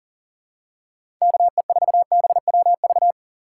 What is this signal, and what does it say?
Morse code, heard as a keyed tone, about 40 wpm, 700 Hz. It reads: KE4BWV
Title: USA Call Signs - 6 Characters - 3x Character Spacing - 40wpm